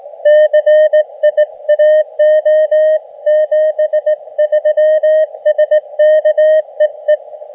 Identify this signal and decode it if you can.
Morse code, heard as a keyed tone, about 19 wpm, 610 Hz. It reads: CIAO 73 SK EE
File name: Expedition / SWL UT5NM 20.12.2012 VANGUARD UA1CEG p IT9IFI